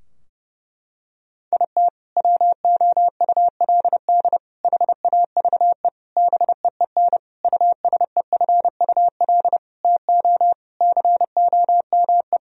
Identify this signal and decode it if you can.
Morse code, heard as a keyed tone, about 30 wpm, 700 Hz. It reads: IT WOULD HAVE BEEN USEFUL TO COME
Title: Sentences from Top 1000 Words - Review Part 4 - 30wpm